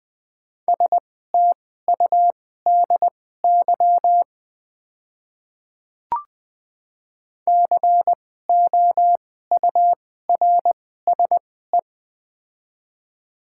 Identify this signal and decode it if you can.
Morse code, heard as a keyed tone, about 20 wpm, spaced as for 14 wpm, 700 Hz. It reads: STUDY COURSE
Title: 2x Character Spacing - Top 500 Words - 20wpm